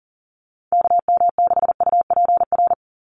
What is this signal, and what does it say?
Morse code, heard as a keyed tone, about 40 wpm, 690 Hz. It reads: KM6UPR
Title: USA Call Signs 40wpm